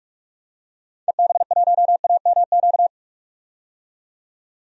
Morse code, heard as a keyed tone, about 43 wpm, 700 Hz. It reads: EB1AMQ
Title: DXCC-100 Call Signs - 6 Characters - 45wpm